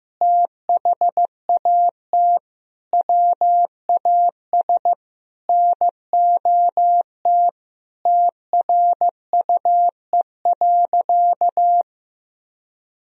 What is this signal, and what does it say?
Morse code, heard as a keyed tone, about 15 wpm, 700 Hz. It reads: THAT WAS NOT TRUE.
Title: Sentences from Top 1000 Words - Review Part 4 - 15wpm